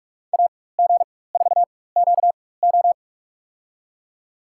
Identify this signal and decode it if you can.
Morse code, heard as a keyed tone, about 46 wpm, spaced as for 21 wpm, 700 Hz. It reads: AG4QO
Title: USA Call Signs - 5 Characters - 4x Character Spacing - 45wpm